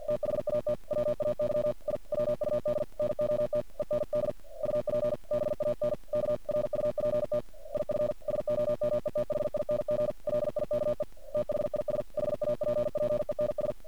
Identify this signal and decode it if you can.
Morse code, heard as a keyed tone, about 50 wpm, 600 Hz, with noise. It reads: THAT WAY I WUD NOT END UP BANKRUPT IF SOMETHING LIKE THIS HAPPENS
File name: archived / CRSnet